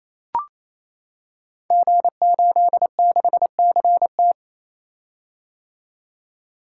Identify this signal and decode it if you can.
Morse code, heard as a keyed tone, about 28 wpm, 700 Hz. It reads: G86CT
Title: Code Groups - 5 Letter:Numbers - 28wpm